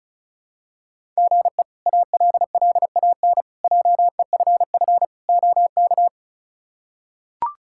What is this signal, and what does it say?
Morse code, heard as a keyed tone, about 34 wpm, 700 Hz. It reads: GE ALLAN JEFF OK
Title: K1USN SST QSO Practice - 35wpm